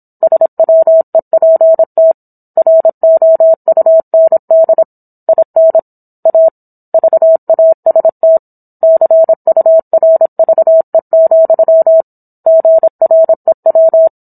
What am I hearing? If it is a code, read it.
Morse code, heard as a keyed tone, about 27 wpm, 650 Hz. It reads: SWEPT ROUND IN A VAST CURVE, GREW